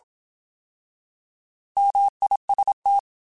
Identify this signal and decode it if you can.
Morse code, heard as a keyed tone, about 27 wpm, 800 Hz. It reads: MIST